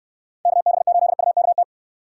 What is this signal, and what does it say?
Morse code, heard as a keyed tone, about 63 wpm, 700 Hz. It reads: KB8UKT